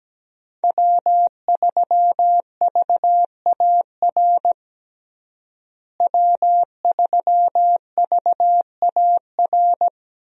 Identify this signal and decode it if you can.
Morse code, heard as a keyed tone, about 17 wpm, 700 Hz. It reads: W3VAR W3VAR
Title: USA Call Signs - 5 Characters - Repeated 3x - 17wpm